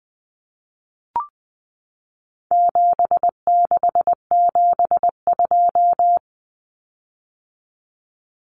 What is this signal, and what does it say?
Morse code, heard as a keyed tone, about 20 wpm, 700 Hz. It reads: 7672
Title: Common Cut Numbers - 4-Digit 20wpm